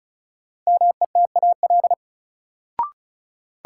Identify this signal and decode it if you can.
Morse code, heard as a keyed tone, about 34 wpm, 700 Hz. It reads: METAL
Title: Sentences from Top 1000 Words - Review Part 6 - New Word Drill - 35wpm